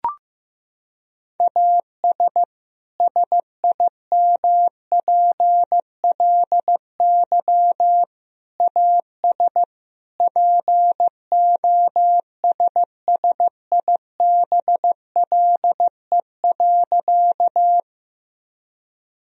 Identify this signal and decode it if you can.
Morse code, heard as a keyed tone, about 15 wpm, 700 Hz. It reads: AS SIMPLY AS POSSIBLE.
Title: Sentences from Top 1000 Words - Review Part 4 - 15wpm